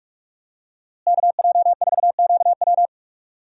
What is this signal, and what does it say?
Morse code, heard as a keyed tone, about 47 wpm, 700 Hz. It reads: KJ4QW
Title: USA Call Signs - 5 Characters - Repeated 3x - 45wpm